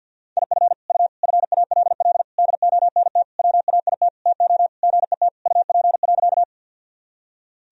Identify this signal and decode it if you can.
Morse code, heard as a keyed tone, about 53 wpm, 700 Hz. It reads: IF U FALL DONT WAIT TO GET UP.
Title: Sentences from Top 500 Words and Abbreviations 50wpm